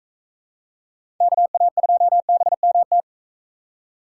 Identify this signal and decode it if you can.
Morse code, heard as a keyed tone, about 43 wpm, 700 Hz. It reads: KA2BMT